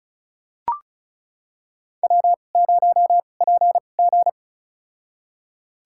Morse code, heard as a keyed tone, about 34 wpm, spaced as for 25 wpm, 700 Hz. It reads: W0PG
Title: USA Call Signs - 2x Character Spacing - 35wpm